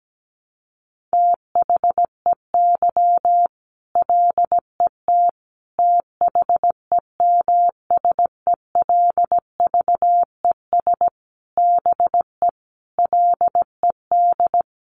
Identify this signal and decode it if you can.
Morse code, heard as a keyed tone, about 17 wpm, 700 Hz. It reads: THEY LET THEMSELVES BE LED
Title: A Sound Of Thunder 17wpm